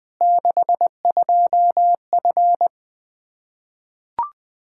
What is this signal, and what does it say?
Morse code, heard as a keyed tone, about 21 wpm, 700 Hz. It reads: 62F
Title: Practice QSO - Exchange 2 - Info - 20wpm